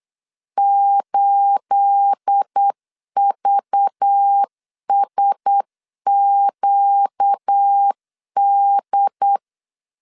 Morse code, heard as a keyed tone, about 8 wpm, 800 Hz. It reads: 8VSQD
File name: LEZIONI / 9 28 8wpm 3512